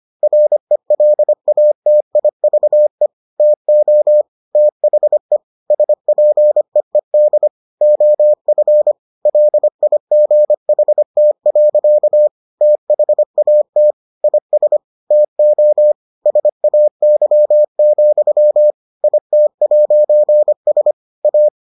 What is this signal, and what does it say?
Morse code, heard as a keyed tone, about 25 wpm, 600 Hz. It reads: RELATIVE TO THE SPEED OF LIGHT. THAT IS TO SAY, IT'S A